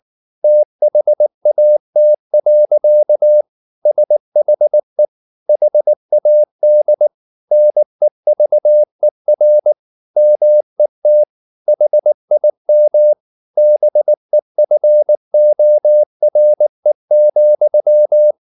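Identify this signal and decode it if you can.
Morse code, heard as a keyed tone, about 19 wpm, 600 Hz. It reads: THAT. SHE HAD NEVER MET HIM BEFORE,
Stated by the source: Morse code at 19 wpm from CW med Gnister, reading Ken Follett's novel Never in English.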